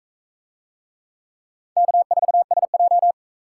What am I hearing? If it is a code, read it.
Morse code, heard as a keyed tone, about 39 wpm, 700 Hz. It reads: K4SJ